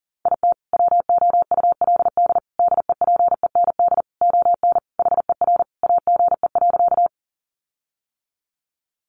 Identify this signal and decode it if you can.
Morse code, heard as a keyed tone, about 40 wpm, 700 Hz. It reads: IT WOULD DEPEND ON HER AGE.
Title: Sentences from Top 1300 Words 40wpm